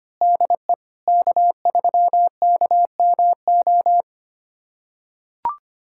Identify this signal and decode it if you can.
Morse code, heard as a keyed tone, about 25 wpm, 700 Hz. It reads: DE K3KMO
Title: USA Callsign DE Callsign 25wpm